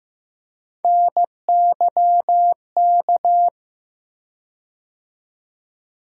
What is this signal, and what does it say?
Morse code, heard as a keyed tone, about 15 wpm, 700 Hz. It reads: NYK
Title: USA Call Sign Postfixes - 3 Letters 15wpm